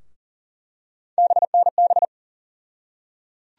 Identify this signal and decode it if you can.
Morse code, heard as a keyed tone, about 40 wpm, 700 Hz. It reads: BNB
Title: USA Call Sign Postfixes - 3 Letters 40wpm